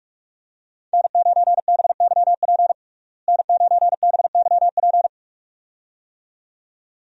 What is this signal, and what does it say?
Morse code, heard as a keyed tone, about 41 wpm, 700 Hz. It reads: N9BYP N9BYP
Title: USA Call Signs - 5 Characters - Repeated 2x - 45wpm